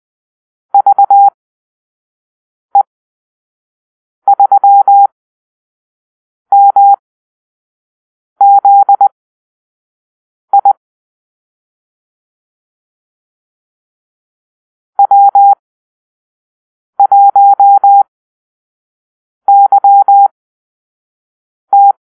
Morse code, heard as a keyed tone, about 20 wpm, spaced as for 5 wpm, 800 Hz. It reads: VE3MZI W1YT
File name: Call-Sign Drill